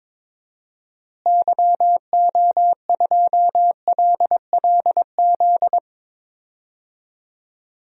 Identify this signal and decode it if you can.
Morse code, heard as a keyed tone, about 21 wpm, 700 Hz. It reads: YO2LLZ
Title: DXCC-100 Call Signs - 6 Characters - Repeated 2x - 22wpm